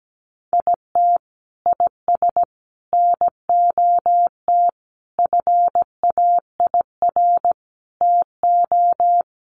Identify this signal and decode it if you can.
Morse code, heard as a keyed tone, about 17 wpm, 700 Hz. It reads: IT IS NOT FAIR TO
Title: Sentences from Top 1300 Words 17wpm